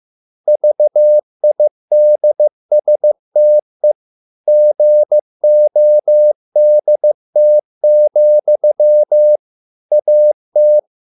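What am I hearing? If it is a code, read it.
Morse code, heard as a keyed tone, about 15 wpm, 600 Hz. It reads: VIDSTE GODT, AT
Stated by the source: Morse code at a slow 15 wpm from CW med Gnister.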